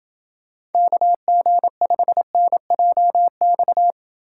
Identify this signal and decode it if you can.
Morse code, heard as a keyed tone, about 28 wpm, 700 Hz. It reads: KG5NJX